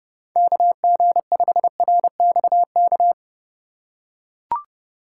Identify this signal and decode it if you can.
Morse code, heard as a keyed tone, about 30 wpm, 700 Hz. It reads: KG5RXK
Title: USA Call Signs - 6 Characters - 3x Character Spacing - 30wpm